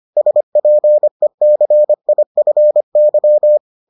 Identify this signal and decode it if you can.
Morse code, heard as a keyed tone, about 25 wpm, 600 Hz. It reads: SPECIFY